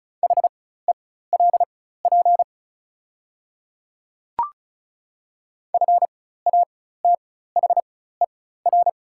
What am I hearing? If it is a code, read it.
Morse code, heard as a keyed tone, about 36 wpm, spaced as for 16 wpm, 700 Hz. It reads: HELP FATHER
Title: 4x Character Spacing - Top 200 Words - 35wpm